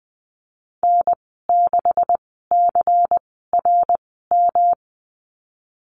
Morse code, heard as a keyed tone, about 20 wpm, spaced as for 15 wpm, 700 Hz. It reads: N6CRM
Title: 128 Lesson 21g - Number 6 - Call Signs 20wpm